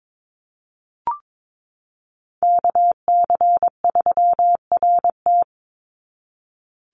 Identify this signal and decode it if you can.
Morse code, heard as a keyed tone, about 22 wpm, 700 Hz. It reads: KC3RT
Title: NA SOTA Chaser Callsigns 22wpm